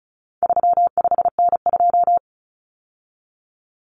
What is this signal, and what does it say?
Morse code, heard as a keyed tone, about 34 wpm, 700 Hz. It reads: 35N2
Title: Common Cut Numbers - 4-Digit 35wpm